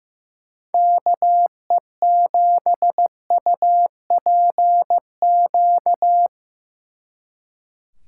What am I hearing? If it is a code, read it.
Morse code, heard as a keyed tone, about 17 wpm, 700 Hz. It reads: KE7UPQ
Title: Speed Racing - USA 2x3 Call Signs 15wpm